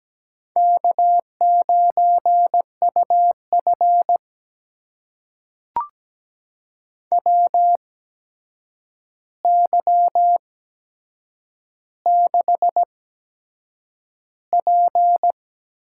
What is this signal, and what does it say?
Morse code, heard as a keyed tone, about 17 wpm, spaced as for 7 wpm, 700 Hz. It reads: K9UF WY6P
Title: USA Call Signs - 4 Characters - 8x Character Spacing - 17wpm